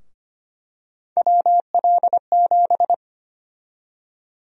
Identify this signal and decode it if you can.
Morse code, heard as a keyed tone, about 25 wpm, 700 Hz. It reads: WL7
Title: USA Call Sign Prefixes - 25wpm